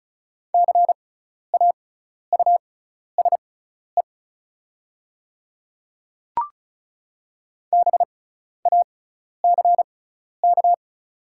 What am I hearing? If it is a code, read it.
Morse code, heard as a keyed tone, about 36 wpm, spaced as for 12 wpm, 700 Hz. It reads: CAUSE BACK
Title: 6x Character Spacing - Top 200 Words - 35wpm